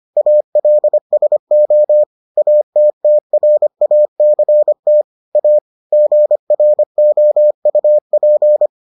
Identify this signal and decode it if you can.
Morse code, heard as a keyed tone, about 25 wpm, 600 Hz. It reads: ALSO ATTRACT A GROUP